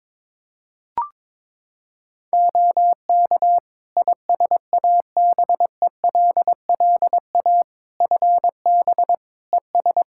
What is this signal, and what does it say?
Morse code, heard as a keyed tone, about 22 wpm, 700 Hz. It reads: OK ISABELLA FB ES
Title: Practice QSO - Exchange 2 - Info - 22wpm